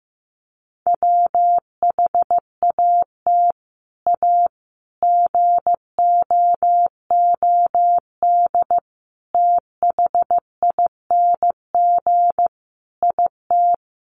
Morse code, heard as a keyed tone, about 15 wpm, 700 Hz. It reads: WHAT A GOOD THING IT
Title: Sentences from Top 200 Words 15wpm